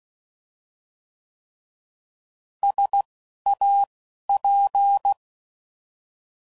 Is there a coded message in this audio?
Yes: Morse code from 2.6 to 5.1 s, heard as a keyed tone, about 16 wpm, spaced as for 12 wpm, 800 Hz. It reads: SAP